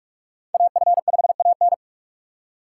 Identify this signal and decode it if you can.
Morse code, heard as a keyed tone, about 46 wpm, 700 Hz. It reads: AF5AN
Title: USA Call Signs - 5 Characters - 2x Character Spacing - 45wpm